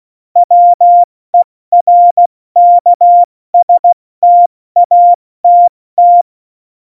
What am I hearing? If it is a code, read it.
Morse code, heard as a keyed tone, about 16 wpm, 700 Hz. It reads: WERKSTATT
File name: Morsekurs / Witze / Witz 07 Juli 2020, 70 F80